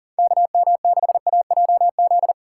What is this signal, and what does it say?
Morse code, heard as a keyed tone, about 40 wpm, 700 Hz. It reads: KM6AJZ
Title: USA Call Signs - 6 Characters - 2x Character Spacing - 40wpm